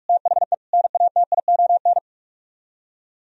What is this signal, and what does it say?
Morse code, heard as a keyed tone, about 44 wpm, 700 Hz. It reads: THE NATION